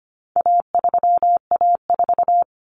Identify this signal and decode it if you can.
Morse code, heard as a keyed tone, about 25 wpm, 700 Hz. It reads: A3A4
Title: Common Cut Numbers - 4-Digit 25wpm